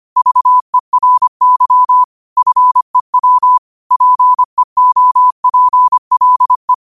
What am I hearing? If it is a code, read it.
Morse code, heard as a keyed tone, about 25 wpm, 1000 Hz. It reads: VERY FEW PEOPLE
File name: qotdfiles / 25wpm